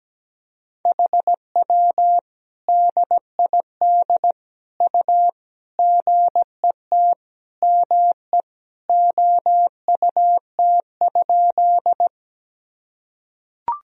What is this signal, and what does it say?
Morse code, heard as a keyed tone, about 17 wpm, 700 Hz. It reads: HW DID U GET ME OUT?
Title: Sentences from Top 200 Words and Abbreviations 17wpm